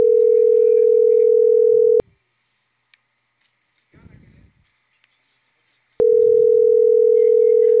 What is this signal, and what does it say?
Ringback tone (North America), 440 + 480 Hz.